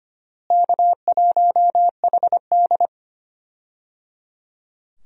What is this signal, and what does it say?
Morse code, heard as a keyed tone, about 25 wpm, 700 Hz. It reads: K1HD
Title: USA Call Signs - 4 Characters - Repeated 3x - 25wpm